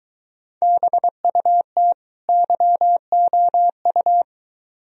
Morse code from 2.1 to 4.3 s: YOU

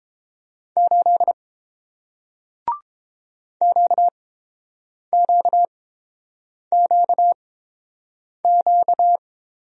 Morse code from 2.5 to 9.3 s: QQQQ

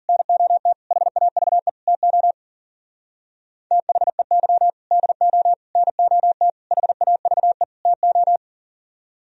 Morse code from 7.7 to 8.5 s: TO

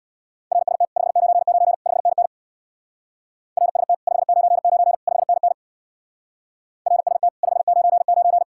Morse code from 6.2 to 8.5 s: RST 599

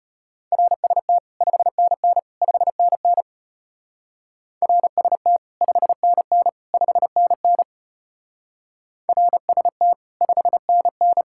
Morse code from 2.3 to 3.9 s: 5NN

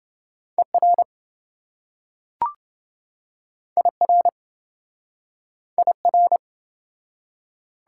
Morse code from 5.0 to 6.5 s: IR

